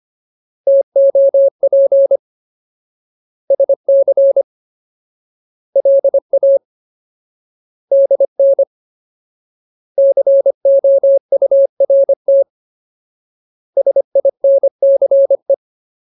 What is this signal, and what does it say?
Morse code, heard as a keyed tone, about 25 wpm, 550 Hz. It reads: TOP SC LA DN COURT SINCE
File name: Ultimate Word List / 25WPM Ultimate